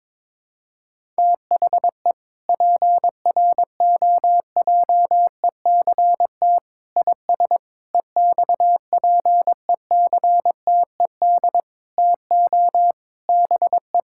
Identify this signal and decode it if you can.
Morse code, heard as a keyed tone, about 22 wpm, 700 Hz. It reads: THE PROJECT IS EXPECTED TO BE
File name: practice / 22wpm / Sentences with 8-Word Phrases - 22wpm